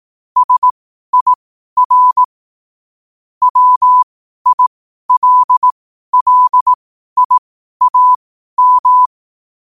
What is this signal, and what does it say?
Morse code, heard as a keyed tone, about 18 wpm, spaced as for 12 wpm, 1000 Hz. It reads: SIR WILLIAM
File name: qotdfiles / 12wpm